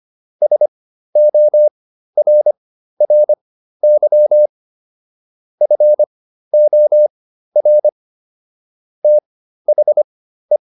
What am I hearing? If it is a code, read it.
Morse code, heard as a keyed tone, about 25 wpm, spaced as for 13 wpm, 600 Hz. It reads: SORRY FOR THE